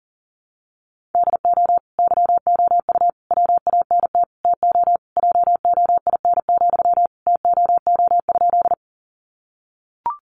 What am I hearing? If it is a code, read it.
Morse code, heard as a keyed tone, about 40 wpm, 700 Hz. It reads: DO YOU WANT TO JOIN, TOO?
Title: Sentences from Top 1400 Words 40wpm